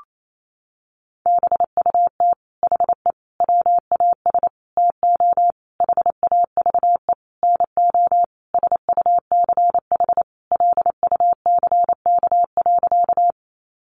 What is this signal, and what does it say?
Morse code, heard as a keyed tone, about 28 wpm, 700 Hz. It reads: BUT HE WAS TO HAVE NO SUCH LUCK.
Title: Sentences from Top 1500 Words 28wpm